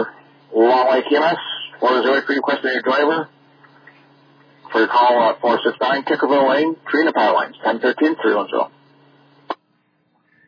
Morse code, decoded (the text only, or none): none